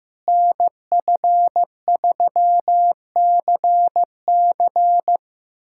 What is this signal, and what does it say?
Morse code, heard as a keyed tone, about 15 wpm, 700 Hz. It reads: NF3CC
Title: USA Callsign DE Callsign 15wpm